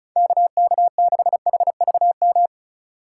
Morse code, heard as a keyed tone, about 34 wpm, 700 Hz. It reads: KK6HVM
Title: USA Call Signs - 6 Characters - 4x Character Spacing - 35wpm